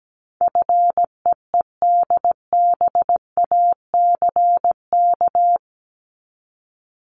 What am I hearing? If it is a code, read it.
Morse code, heard as a keyed tone, about 17 wpm, 700 Hz. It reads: FEEDBACK
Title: Top 1501-1750 Words 17wpm